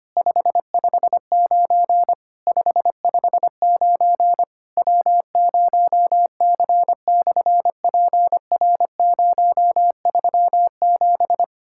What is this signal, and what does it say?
Morse code, heard as a keyed tone, about 25 wpm, 700 Hz. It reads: 559 559 W0C/PR037
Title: SOTA QSO Practice 25wpm